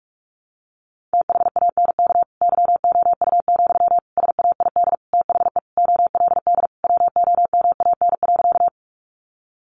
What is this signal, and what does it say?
Morse code, heard as a keyed tone, about 44 wpm, 700 Hz. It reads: THANK YOU, SAID THE OLD WOMAN.
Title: Sentences from Top 1100 Words 45wpm